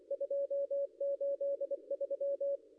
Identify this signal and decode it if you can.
Morse code, heard as a keyed tone, about 24 wpm, 560 Hz. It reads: 283
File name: contests / ss-cw-k5tr-op-n5rz -0700